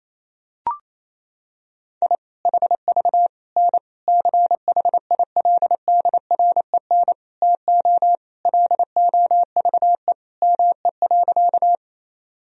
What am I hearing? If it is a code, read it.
Morse code, heard as a keyed tone, about 28 wpm, 700 Hz. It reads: I HV N CHILDREN TO LOVE ME.
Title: Sentences from Top 400 Words and Abbreviations 28wpm